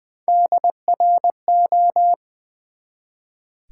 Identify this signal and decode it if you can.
Morse code, heard as a keyed tone, about 20 wpm, 700 Hz. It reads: DRO
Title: Build A Word - 1 Letter At A Time - 4-Letter Words - 20wpm